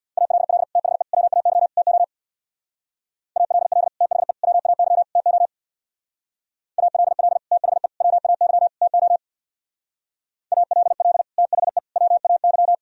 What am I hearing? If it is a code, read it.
Morse code, heard as a keyed tone, about 64 wpm, 700 Hz. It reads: ALL THE WAY TO ALL THE WAY TO ALL THE WAY TO ALL THE WAY